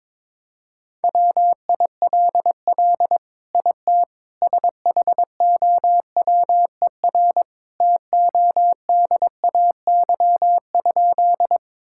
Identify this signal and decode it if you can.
Morse code, heard as a keyed tone, about 22 wpm, 700 Hz. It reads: WILL IT SHOWER TODAY?
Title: Sentences from Top 1600 Words 22wpm